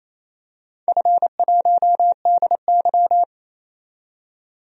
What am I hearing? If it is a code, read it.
Morse code, heard as a keyed tone, about 28 wpm, 700 Hz. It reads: F1DY
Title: DXCC-100 Call Signs - 4 Characters - 28wpm